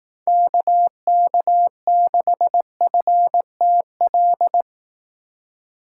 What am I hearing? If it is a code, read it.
Morse code, heard as a keyed tone, about 18 wpm, 700 Hz. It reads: KK6FTL
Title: Speed Racing - USA 2x3 Call Signs 15wpm